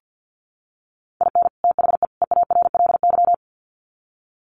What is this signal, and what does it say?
Morse code, heard as a keyed tone, about 50 wpm, 700 Hz. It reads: IN THE EARLY
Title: Sets of 3 Words 50wpm